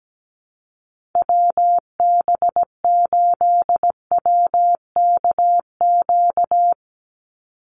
Morse code, heard as a keyed tone, about 17 wpm, 700 Hz. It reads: WB8WKQ
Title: Call Signs - 17wpm